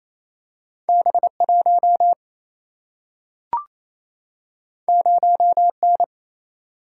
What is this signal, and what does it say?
Morse code, heard as a keyed tone, about 28 wpm, 700 Hz. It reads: B 1 0 N
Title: Single-Letter Single-Number 28wpm